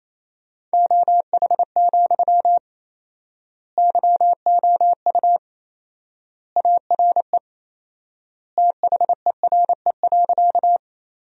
Morse code, heard as a keyed tone, about 28 wpm, 700 Hz. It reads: OH, YOU ARE THERE.
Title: 4x Word-Spacing - Sentences from Top 500 Words - 28wpm